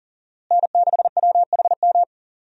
Morse code, heard as a keyed tone, about 37 wpm, 700 Hz. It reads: N6WHM
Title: USA Call Signs - 5 Characters - Repeated 2x - 40wpm